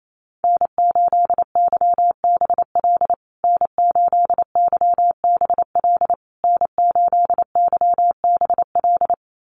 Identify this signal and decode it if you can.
Morse code, heard as a keyed tone, about 28 wpm, 700 Hz. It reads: N8YBL N8YBL N8YBL
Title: USA Call Signs Repeated 3x 28wpm